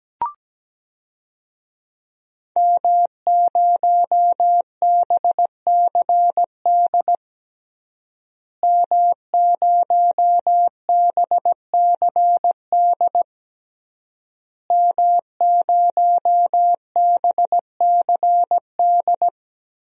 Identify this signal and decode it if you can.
Morse code, heard as a keyed tone, about 17 wpm, 700 Hz. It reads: M0BCD M0BCD M0BCD
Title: DXCC-100 Call Signs - 5 Characters - Repeated 3x - 17wpm